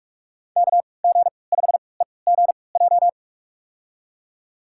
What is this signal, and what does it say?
Morse code, heard as a keyed tone, about 46 wpm, spaced as for 26 wpm, 700 Hz. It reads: KG5EGJ
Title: USA Call Signs - 3x Character Spacing - 45wpm